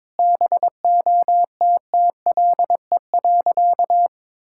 Morse code, heard as a keyed tone, about 22 wpm, 700 Hz. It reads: BOTTLE.
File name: practice / 22wpm / Sentences with 4-Word Phrases - 22wpm